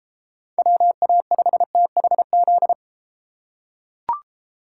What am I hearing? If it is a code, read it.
Morse code, heard as a keyed tone, about 31 wpm, 700 Hz. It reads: WA5THZ